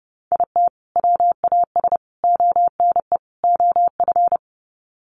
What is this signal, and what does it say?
Morse code, heard as a keyed tone, about 30 wpm, 700 Hz. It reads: IT WAS ONE OF
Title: Sets of 4 Words 30wpm